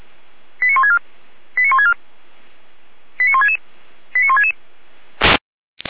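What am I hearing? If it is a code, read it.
Five-tone selective call (ZVEI), heard as quick runs of five tones, tones of about 70 ms. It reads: 89164 89164 89166 89166
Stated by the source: ZVEI five-tone selective call received via SDR, fire brigade Feuerwehr Orlamünde.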